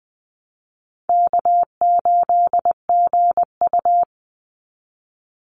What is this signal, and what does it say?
Morse code, heard as a keyed tone, about 20 wpm, 700 Hz. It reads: K8GU
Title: Easy Call Signs Repeated 3x - 20wpm